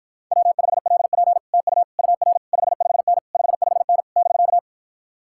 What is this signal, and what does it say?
Morse code, heard as a keyed tone, about 53 wpm, 700 Hz. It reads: W5LP TU UR 55N 55N <BK>